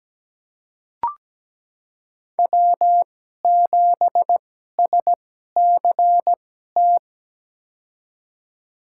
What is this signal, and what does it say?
Morse code, heard as a keyed tone, about 17 wpm, spaced as for 12 wpm, 700 Hz. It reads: W7SCT
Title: USA Call Signs - 5 Characters - 2x Character Spacing - 17wpm